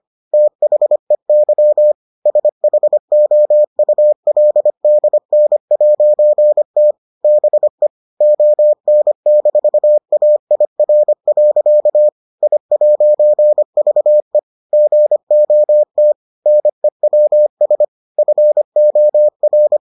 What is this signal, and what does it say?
Morse code, heard as a keyed tone, about 25 wpm, 600 Hz. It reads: THEY SHOULDN'T BE ON-AIR. I'VE GOT NEWS FOR